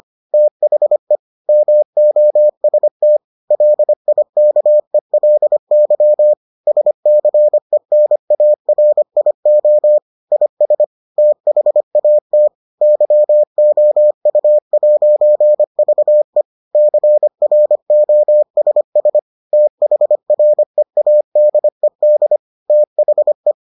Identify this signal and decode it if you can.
Morse code, heard as a keyed tone, about 25 wpm, 600 Hz. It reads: THE MOST LIKELY SCENARIO IS THAT YOU'VE CROSS THREADED THE